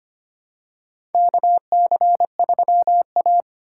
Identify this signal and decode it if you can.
Morse code, heard as a keyed tone, about 25 wpm, 700 Hz. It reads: KC3A
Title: USA Call Signs - 4x Character Spacing - 25wpm